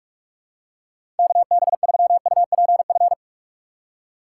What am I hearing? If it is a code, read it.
Morse code, heard as a keyed tone, about 46 wpm, 700 Hz. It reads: KB3UPF
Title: USA Call Signs - 6 Characters - 8x Character Spacing - 45wpm